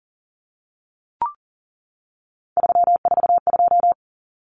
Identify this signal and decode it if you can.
Morse code, heard as a keyed tone, about 40 wpm, 700 Hz. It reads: 342
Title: Numbers - 3-Digit 40wpm